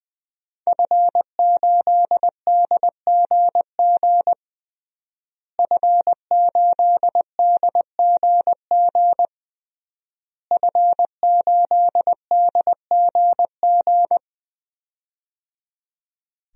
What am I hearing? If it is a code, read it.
Morse code, heard as a keyed tone, about 21 wpm, 700 Hz. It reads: F8DGG F8DGG F8DGG
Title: DXCC-100 Call Signs - 5 Characters - Repeated 3x - 20wpm